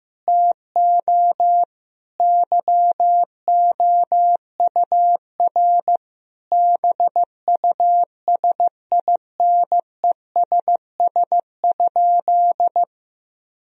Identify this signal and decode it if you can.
Morse code, heard as a keyed tone, about 15 wpm, 700 Hz. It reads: TO YOUR BUSINESS?
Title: Sentences from Top 1000 Words - Review Part 4 - 15wpm